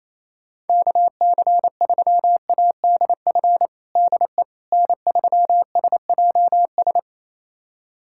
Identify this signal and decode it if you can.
Morse code, heard as a keyed tone, about 28 wpm, 700 Hz. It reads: KC3ADF DE N3SJS
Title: USA Callsign DE Callsign 28wpm